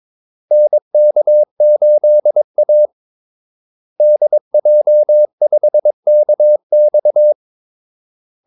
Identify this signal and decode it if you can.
Morse code, heard as a keyed tone, about 22 wpm, 600 Hz. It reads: NK8A DJ5KX
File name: Callsigns-22wpm